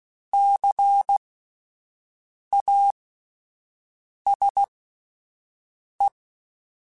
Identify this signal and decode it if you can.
Morse code, heard as a keyed tone, about 16 wpm, spaced as for 5 wpm, 800 Hz. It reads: CASE